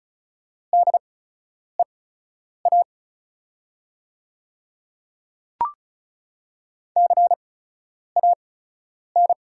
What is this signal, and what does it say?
Morse code, heard as a keyed tone, about 34 wpm, spaced as for 10 wpm, 700 Hz. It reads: DEA CAN
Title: 8x Character Spacing - Top 300 Words - 35wpm